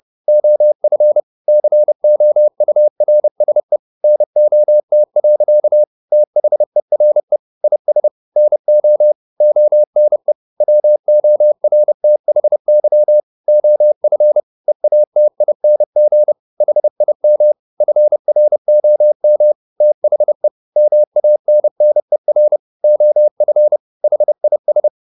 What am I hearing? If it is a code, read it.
Morse code, heard as a keyed tone, about 30 wpm, 600 Hz. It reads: OF COURSE NOT. THERE IS NO ONE WORTHY OF EATING HIM FROM THE MANNER OF HIS